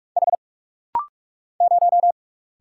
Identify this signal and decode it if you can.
Morse code, heard as a keyed tone, about 45 wpm, 700 Hz. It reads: H 0
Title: Single Letter-Number - Rapid-Fire 45wpm